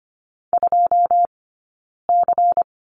Morse code, heard as a keyed tone, about 25 wpm, 700 Hz. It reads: 2 C
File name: course / 25:10wpm / 092 Lesson 16d - Letter c - Two Characters 25wpm